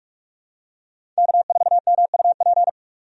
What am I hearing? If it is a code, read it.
Morse code, heard as a keyed tone, about 46 wpm, 700 Hz. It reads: K4MUP
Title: USA Call Signs - 5 Characters - 45wpm